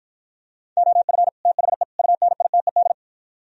Morse code, heard as a keyed tone, about 52 wpm, 700 Hz. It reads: OF THE UNITED